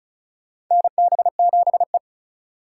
Morse code, heard as a keyed tone, about 36 wpm, 700 Hz. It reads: NB7E
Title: USA Call Signs - 4 Characters - 3x Character Spacing - 35wpm